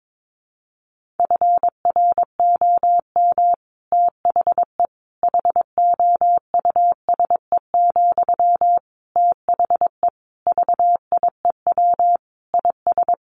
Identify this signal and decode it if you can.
Morse code, heard as a keyed tone, about 22 wpm, 700 Hz. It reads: FROM THE HOUSE, THE VIEW IS
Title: Sentences from Top 1100 Words 22wpm